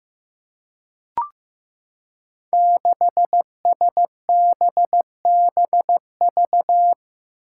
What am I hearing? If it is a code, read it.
Morse code, heard as a keyed tone, about 15 wpm, 700 Hz. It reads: 6SBBV
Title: Five Challenging Letter-Number Combinations 15wpm